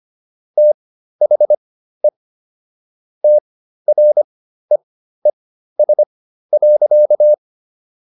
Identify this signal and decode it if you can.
Morse code, heard as a keyed tone, about 25 wpm, spaced as for 13 wpm, 600 Hz. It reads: THE TREES.